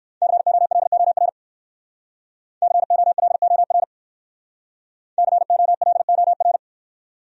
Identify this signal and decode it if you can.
Morse code, heard as a keyed tone, about 59 wpm, 700 Hz. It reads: COLOR COLOR COLOR